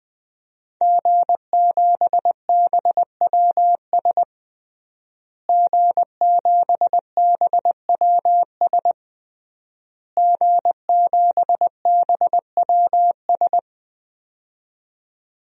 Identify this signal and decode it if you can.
Morse code, heard as a keyed tone, about 20 wpm, 700 Hz. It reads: G7BWS G7BWS G7BWS
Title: DXCC-100 Call Signs - 5 Characters - Repeated 3x - 20wpm